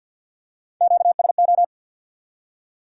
Morse code, heard as a keyed tone, about 45 wpm, 700 Hz. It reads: QSO
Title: Top 180 QSO Elements 50wpm